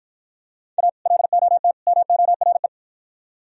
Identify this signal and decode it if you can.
Morse code, heard as a keyed tone, about 53 wpm, 700 Hz. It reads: A LOT MORE